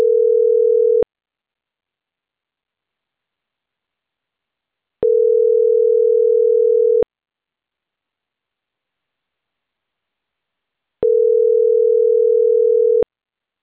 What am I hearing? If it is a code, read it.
Ringback tone (North America), 440 + 480 Hz.